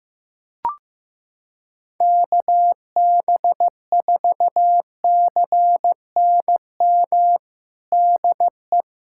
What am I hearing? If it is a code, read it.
Morse code, heard as a keyed tone, about 15 wpm, 700 Hz. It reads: KB4CNM DE
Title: USA Callsign DE Callsign 15wpm